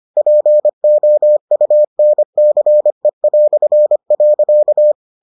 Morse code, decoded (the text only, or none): POUNCE".